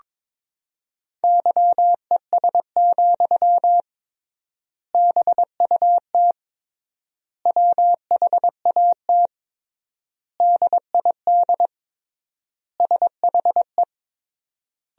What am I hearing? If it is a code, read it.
Morse code, heard as a keyed tone, about 22 wpm, 700 Hz. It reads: YES, BUT WHAT DID SHE
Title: 3x Word-Spacing - Sentences from Top 500 Words - 22wpm